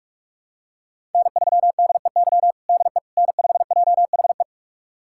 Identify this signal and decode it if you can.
Morse code, heard as a keyed tone, about 46 wpm, 700 Hz. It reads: N3DEY DE N5JHE
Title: USA Callsign DE Callsign - 5 Characters - 45wpm